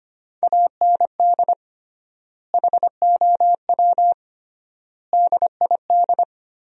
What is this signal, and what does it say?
Morse code, heard as a keyed tone, about 25 wpm, 700 Hz. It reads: AND HOW DID
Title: 3x Word-Spacing - Sentences from Top 100 Words - 25wpm